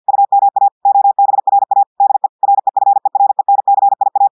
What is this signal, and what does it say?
Morse code, heard as a keyed tone, about 50 wpm, 800 Hz. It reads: UMA OBRA DE REFERENCIA
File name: audios / portugais / portugais texte en clair 1 vitesse 50 mots minutes